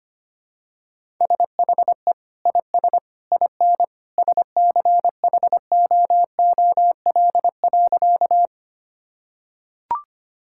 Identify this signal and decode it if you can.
Morse code, heard as a keyed tone, about 25 wpm, 700 Hz. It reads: SHE IS IN SCHOOL.
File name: practice / 25wpm / Sentences with 2-Word Phrases - 25wpm